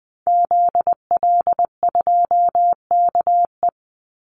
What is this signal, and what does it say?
Morse code, heard as a keyed tone, about 20 wpm, 700 Hz. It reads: ZL2KE
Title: Call Signs - 20wpm